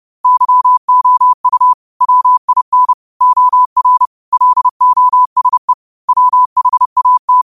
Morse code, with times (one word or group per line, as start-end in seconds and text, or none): 0.2-1.7 YOU
2.0-2.9 WIN
3.2-4.1 OR
4.3-5.7 LOSE
6.1-7.4 WHAT